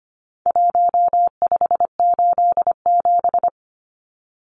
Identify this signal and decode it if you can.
Morse code, heard as a keyed tone, about 25 wpm, 700 Hz. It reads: 1587
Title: Numbers - Contest Style - 4-Digit 25wpm